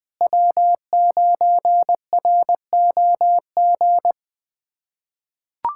Morse code, heard as a keyed tone, about 21 wpm, 700 Hz. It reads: W9ROG
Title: USA Callsign DE Callsign - 5 Characters - 20wpm